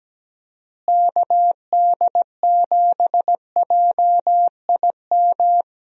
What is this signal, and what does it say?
Morse code, heard as a keyed tone, about 17 wpm, 700 Hz. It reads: KD7JIM
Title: USA Call Signs - 6 Characters - Repeated 3x - 17wpm